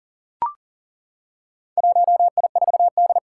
Morse code, heard as a keyed tone, about 40 wpm, 700 Hz. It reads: 1I4D
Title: Four Challenging Letter-Number Combinations 40wpm